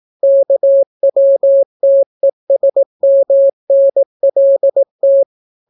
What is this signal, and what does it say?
Morse code, heard as a keyed tone, about 18 wpm, 550 Hz. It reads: KWTESMNLT